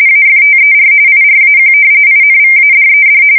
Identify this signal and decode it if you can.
RTTY, about 45.45 baud. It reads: AFTER IRAQ INVADED
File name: fsk170 / battleshipnj